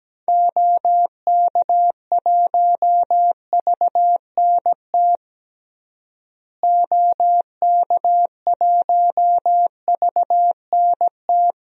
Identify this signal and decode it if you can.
Morse code, heard as a keyed tone, about 17 wpm, 700 Hz. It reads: OK1VNT OK1VNT
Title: DXCC-100 Call Signs - 6 Characters - Repeated 2x - 17wpm